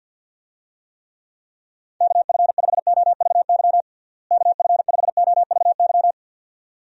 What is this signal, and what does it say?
Morse code, heard as a keyed tone, about 50 wpm, 700 Hz. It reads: KF5OVY KF5OVY
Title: USA Call Signs - 6 Characters - Repeated 2x - 50wpm